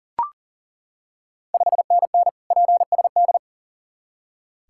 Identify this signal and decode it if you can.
Morse code, heard as a keyed tone, about 40 wpm, 700 Hz. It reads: 5NN PSD
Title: Ontario QSO Party Exchange 40wpm